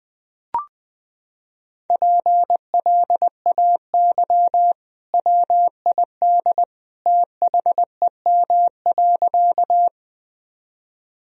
Morse code, heard as a keyed tone, about 20 wpm, 700 Hz. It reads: PLAY WID THEM.
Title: Sentences from Top 200 Words and Abbreviations 20wpm